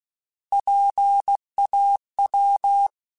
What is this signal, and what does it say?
Morse code, heard as a keyed tone, about 16 wpm, 800 Hz. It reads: PAW
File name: Alphabetical / Alphabetical mp3s / Ps @15